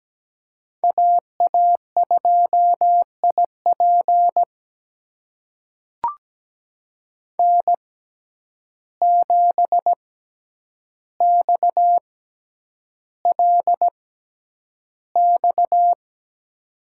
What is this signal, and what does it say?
Morse code, heard as a keyed tone, about 17 wpm, spaced as for 9 wpm, 700 Hz. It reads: AA2IP N7XLX
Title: USA Call Signs - 5 Characters - 6x Character Spacing - 17wpm